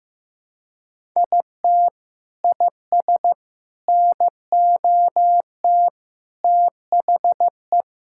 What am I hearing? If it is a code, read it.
Morse code, heard as a keyed tone, about 15 wpm, 700 Hz. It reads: IT IS NOT THE
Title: Sentences from Top 1000 Words - Review Part 6 - 15wpm